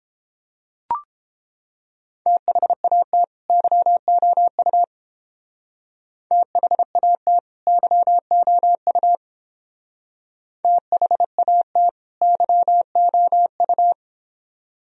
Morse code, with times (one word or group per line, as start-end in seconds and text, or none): 2.3-3.2 THAT
3.5-4.8 YOU
6.3-7.4 THAT
7.7-9.2 YOU
10.6-11.9 THAT
12.2-13.9 YOU